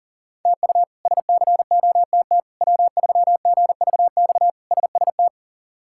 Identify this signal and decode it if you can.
Morse code, heard as a keyed tone, about 37 wpm, 700 Hz. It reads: TU SCOTT W3GVX SST
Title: K1USN SST QSO Practice - 40wpm